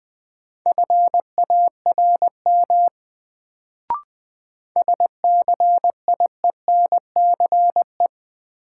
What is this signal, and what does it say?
Morse code, heard as a keyed tone, about 20 wpm, 700 Hz. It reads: FARM SCIENCE
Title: Sentences from Top 1000 Words - Review Part 2 - New Word Drill - 20wpm